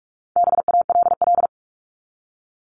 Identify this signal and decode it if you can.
Morse code, heard as a keyed tone, about 44 wpm, 700 Hz. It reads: BALL
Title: Top 751-1000 Words 45wpm